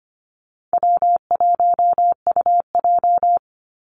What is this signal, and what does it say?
Morse code, heard as a keyed tone, about 25 wpm, 700 Hz. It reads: W1UJ
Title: Call Signs Repeated 3x - 25wpm